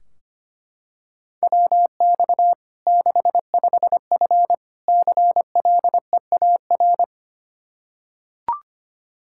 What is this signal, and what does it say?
Morse code, heard as a keyed tone, about 25 wpm, 700 Hz. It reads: WX 65F CLEAR
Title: Practice QSO - Exchange 2 - Info - 25wpm